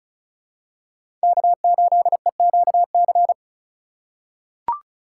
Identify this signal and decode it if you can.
Morse code, heard as a keyed tone, about 36 wpm, 700 Hz. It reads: K8EQC
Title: USA Call Signs - 5 Characters - 6x Character Spacing - 35wpm